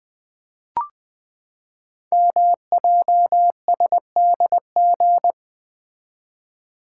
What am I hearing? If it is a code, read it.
Morse code, heard as a keyed tone, about 20 wpm, 700 Hz. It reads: MJSDG
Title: Code Groups - 5 Letters - 20wpm